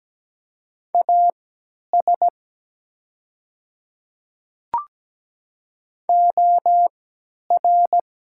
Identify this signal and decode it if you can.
Morse code, heard as a keyed tone, about 17 wpm, spaced as for 10 wpm, 700 Hz. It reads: AS OR
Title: 3x Character Spacing - Top 100 Words - 17wpm